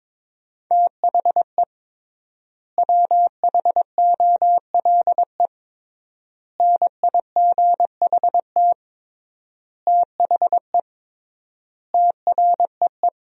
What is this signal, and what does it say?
Morse code, heard as a keyed tone, about 22 wpm, 700 Hz. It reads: THE WHOLE NIGHT THE TREE
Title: 3x Word-Spacing - Sentences from Top 500 Words - 22wpm